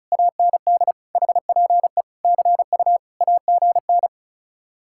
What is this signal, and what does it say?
Morse code, heard as a keyed tone, about 43 wpm, 700 Hz. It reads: AND HPE CU AGN
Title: Speed-Racing Common QSO Phrases - 35wpm